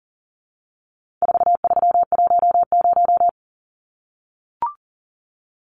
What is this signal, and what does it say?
Morse code, heard as a keyed tone, about 39 wpm, 700 Hz. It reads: 4310
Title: Numbers - 4-Digit 40wpm